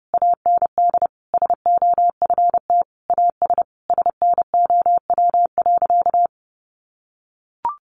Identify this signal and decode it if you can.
Morse code, heard as a keyed tone, about 30 wpm, 700 Hz. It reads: AND SOFT AS SNOW.
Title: Sentences from Top 1500 Words 30wpm